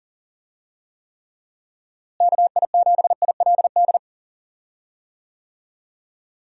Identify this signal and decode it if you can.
Morse code, heard as a keyed tone, about 40 wpm, 700 Hz. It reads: KI7ILD
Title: USA Call Signs - 6 Characters - Repeated 3x - 40wpm